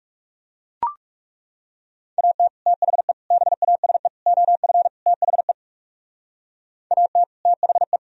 Morse code, heard as a keyed tone, about 38 wpm, 700 Hz. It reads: AT THE BASE OF THE AT THE